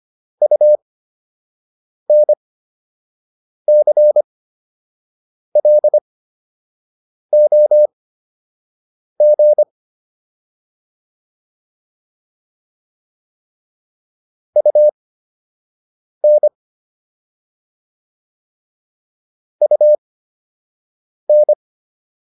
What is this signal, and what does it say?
Morse code, heard as a keyed tone, about 25 wpm, spaced as for 6 wpm, 600 Hz. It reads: UNCLOG UN UN